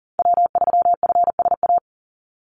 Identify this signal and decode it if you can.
Morse code, heard as a keyed tone, about 40 wpm, 700 Hz. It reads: W3FSA
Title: Call Signs Repeated 2x - 40wpm